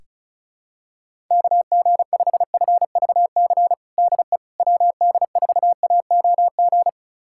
Morse code, heard as a keyed tone, about 36 wpm, 700 Hz. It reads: KG5FVC DE WD4AOG
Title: USA Callsign DE Callsign 35wpm